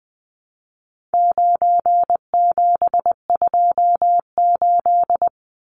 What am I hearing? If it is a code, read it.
Morse code, heard as a keyed tone, about 20 wpm, 700 Hz. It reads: 9728
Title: Numbers - 4-Digit 20wpm